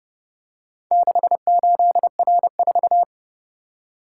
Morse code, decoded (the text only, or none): 68R4